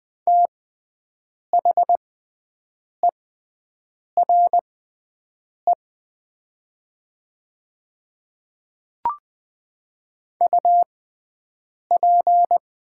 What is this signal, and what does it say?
Morse code, heard as a keyed tone, about 20 wpm, spaced as for 7 wpm, 700 Hz. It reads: THERE UP
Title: 6x Character Spacing - Top 100 Words - 20wpm